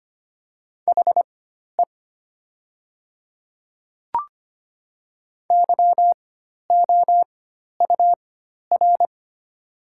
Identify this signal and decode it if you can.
Morse code, heard as a keyed tone, about 25 wpm, spaced as for 12 wpm, 700 Hz. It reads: HE YOUR
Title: 4x Character Spacing - Top 100 Words - 25wpm